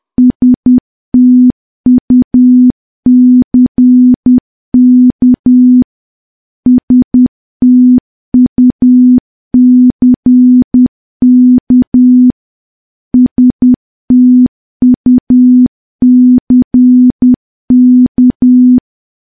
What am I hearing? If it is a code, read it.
Morse code, heard as a keyed tone, about 10 wpm, 250 Hz. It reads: STUCK STUCK STUCK
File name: morse / morse-training-10WPM